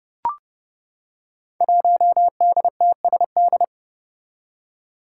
Morse code, heard as a keyed tone, about 30 wpm, 700 Hz. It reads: 1DTSD
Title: Five Challenging Letter-Number Combinations 30wpm